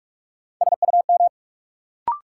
Morse code, heard as a keyed tone, about 45 wpm, 700 Hz. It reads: SUM
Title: Sentences from Top 1000 Words - Review Part 6 - New Word Drill - 45wpm